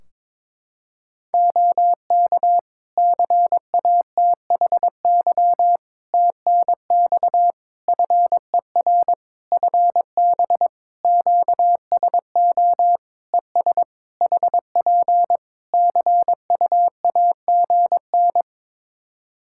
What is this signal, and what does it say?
Morse code, heard as a keyed tone, about 22 wpm, 700 Hz. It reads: OK CATHY TNX FER FB QSO ES HP CUAGN
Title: K1USN SST QSO Practice 22wpm